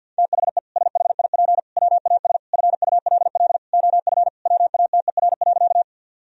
Morse code, heard as a keyed tone, about 50 wpm, 700 Hz. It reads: THE SHIP WAS FULL OF WATER.
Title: Sentences from Top 1000 Words - Review Part 3 - 50wpm